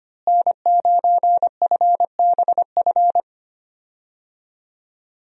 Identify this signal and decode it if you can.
Morse code, heard as a keyed tone, about 25 wpm, 700 Hz. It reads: N9FBF
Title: USA Call Signs - 5 Characters - Repeated 2x - 25wpm